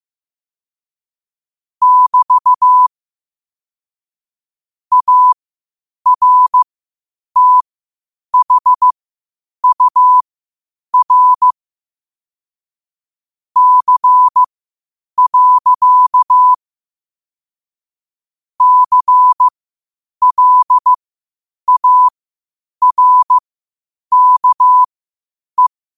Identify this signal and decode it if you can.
Morse code, heard as a keyed tone, about 15 wpm, spaced as for 8 wpm, 1000 Hz. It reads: <BT> ARTHUR C. CLARKE
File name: qotdfiles / 8wpm